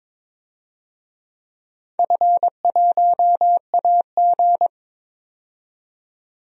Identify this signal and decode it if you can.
Morse code, heard as a keyed tone, about 21 wpm, 700 Hz. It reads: F1AG
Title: DXCC-100 Call Signs - 4 Characters - 22wpm